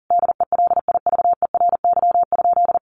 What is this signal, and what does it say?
Morse code, heard as a keyed tone, about 40 wpm, 700 Hz. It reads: DELIVERY?
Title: Sentences from Top 1500 Words 40wpm